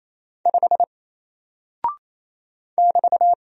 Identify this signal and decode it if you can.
Morse code, heard as a keyed tone, about 27 wpm, 700 Hz. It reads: 5 <BT>
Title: Single Character (Letter:Number:Punctuation:Prosign) Alt 28wpm